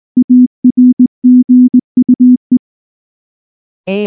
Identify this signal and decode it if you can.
Morse code, heard as a keyed tone, about 19 wpm, 250 Hz. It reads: ARGUE
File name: morse / morse-training-20WPM-6